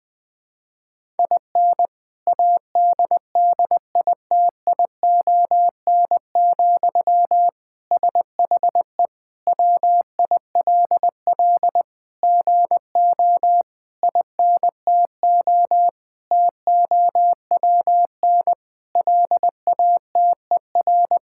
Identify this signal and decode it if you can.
Morse code, heard as a keyed tone, about 19 wpm, 700 Hz. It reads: IN ADDITION, SHE WILL GO INTO TOWN LATER
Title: Sentences from Top 1000 Words - Review Part 6 - 20wpm